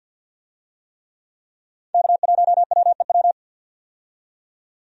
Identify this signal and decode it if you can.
Morse code, heard as a keyed tone, about 48 wpm, 700 Hz. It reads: K1WEW
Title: USA Call Signs - 5 Characters - 50wpm